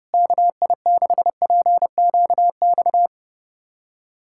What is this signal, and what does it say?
Morse code, heard as a keyed tone, about 30 wpm, 700 Hz. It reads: KI6PQX
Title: USA Callsign DE Callsign - 6 Characters - 30wpm